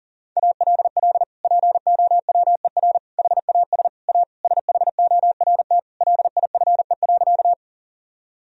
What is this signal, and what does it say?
Morse code, heard as a keyed tone, about 37 wpm, 700 Hz. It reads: ALL POWER HAS A SHORT LIFE.
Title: Sentences from Top 1000 Words - Review Part 2 - 40wpm